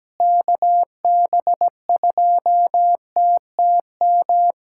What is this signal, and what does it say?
Morse code, heard as a keyed tone, about 17 wpm, 700 Hz. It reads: KB2TTM
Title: USA Callsign DE Callsign - 6 Characters - 17wpm